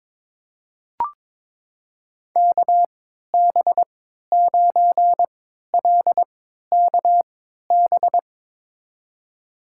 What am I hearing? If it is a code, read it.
Morse code, heard as a keyed tone, about 22 wpm, spaced as for 13 wpm, 700 Hz. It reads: KB9LKB
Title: USA Call Signs - 6 Characters - 3x Character Spacing - 22wpm